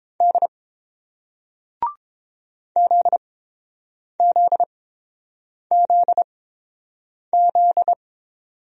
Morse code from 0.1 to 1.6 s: D

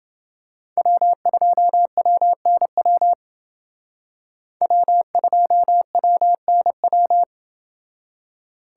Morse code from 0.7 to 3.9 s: W2WNW

